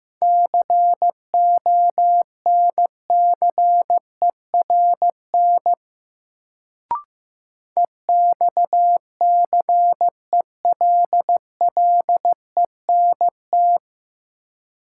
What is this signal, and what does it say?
Morse code, heard as a keyed tone, about 15 wpm, 700 Hz. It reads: CONCERN EXCELLENT
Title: Sentences from Top 1000 Words - New Word Drill - 15wpm